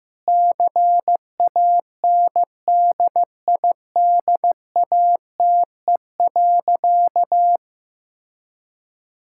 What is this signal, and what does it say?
Morse code, heard as a keyed tone, about 15 wpm, 700 Hz. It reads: CANDIDATE.
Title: Sentences from Top 1600 Words 15wpm